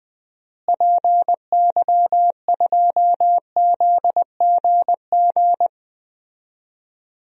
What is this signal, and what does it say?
Morse code, heard as a keyed tone, about 20 wpm, 700 Hz. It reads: PY2ZGG
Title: DXCC-100 Call Signs - 6 Characters - Repeated 2x - 20wpm